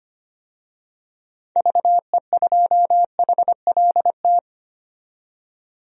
Morse code, heard as a keyed tone, about 25 wpm, 700 Hz. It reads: VE2HLT
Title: DXCC-100 Call Signs - 6 Characters - 25wpm